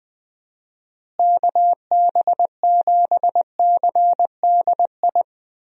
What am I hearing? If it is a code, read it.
Morse code, heard as a keyed tone, about 21 wpm, 700 Hz. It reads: KB7CDI
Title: USA Call Signs - 6 Characters - Repeated 3x - 20wpm